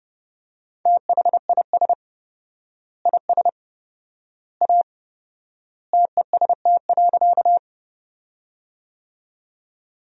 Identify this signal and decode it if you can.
Morse code, heard as a keyed tone, about 30 wpm, 700 Hz. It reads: THIS IS A TEST.
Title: 4x Word-Spacing - Sentences from Top 500 Words - 30wpm